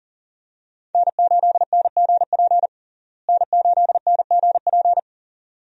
Morse code, heard as a keyed tone, about 37 wpm, 700 Hz. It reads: N8NGP N8NGP
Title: USA Call Signs - 5 Characters - Repeated 2x - 40wpm